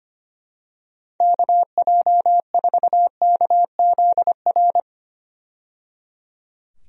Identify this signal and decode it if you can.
Morse code, heard as a keyed tone, about 25 wpm, 700 Hz. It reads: KJ4KZR
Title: USA Call Signs - 6 Characters - 25wpm